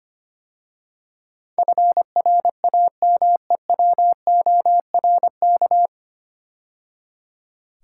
Morse code, heard as a keyed tone, about 25 wpm, 700 Hz. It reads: FRAMEWORK
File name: practice / 25wpm / 9 Letter Words - First 3 Letters Given - 25wpm